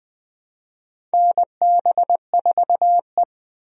Morse code, heard as a keyed tone, about 21 wpm, 700 Hz. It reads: NB4E
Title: USA Call Signs - 4 Characters - Repeated 2x - 20wpm